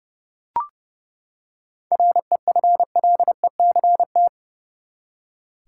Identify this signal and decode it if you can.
Morse code, heard as a keyed tone, about 30 wpm, 700 Hz. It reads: REFLECT
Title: Seven Letter Words - 30wpm